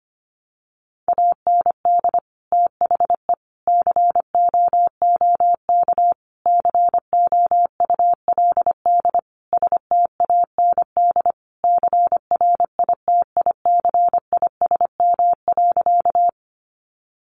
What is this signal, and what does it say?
Morse code, heard as a keyed tone, about 25 wpm, 700 Hz. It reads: AND THE COOK COULD STAND CRITICISM.
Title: Sentences from Top 1300 Words 25wpm